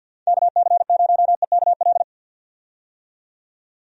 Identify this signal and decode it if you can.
Morse code, heard as a keyed tone, about 51 wpm, 700 Hz. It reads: KC0EKL
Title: USA Call Signs - 6 Characters - 50wpm